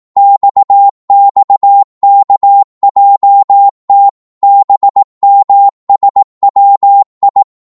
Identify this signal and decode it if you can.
Morse code, heard as a keyed tone, about 18 wpm, 800 Hz. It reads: XXKJT BMSWI
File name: Morse / ltr-18wpm